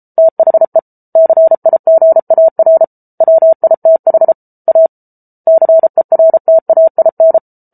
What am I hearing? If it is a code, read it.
Morse code, heard as a keyed tone, about 34 wpm, 650 Hz. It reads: THE CIGAR WITH A CERTAIN